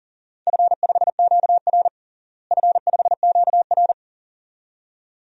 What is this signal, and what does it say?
Morse code, heard as a keyed tone, about 40 wpm, 700 Hz. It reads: F5QR F5QR
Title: DXCC-100 Call Signs - 4 Characters - Repeated 2x - 40wpm